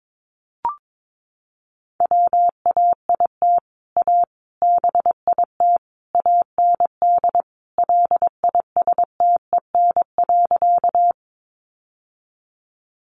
Morse code, heard as a keyed tone, about 22 wpm, 700 Hz. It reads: WAIT A BIT AND LISTEN.
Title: Sentences from Top 900 Words 22wpm